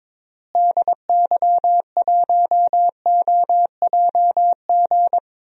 Morse code, heard as a keyed tone, about 22 wpm, 700 Hz. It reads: DY1OJG
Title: DXCC-100 Call Signs - 6 Characters - Repeated 2x - 22wpm